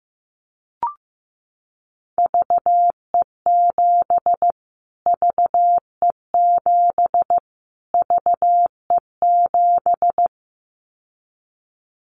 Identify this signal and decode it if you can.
Morse code, heard as a keyed tone, about 15 wpm, 700 Hz. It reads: VE7 VE7 VE7
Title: Callsign Prefixes for Top 25 DXCC - Repeated 3x - 15wpm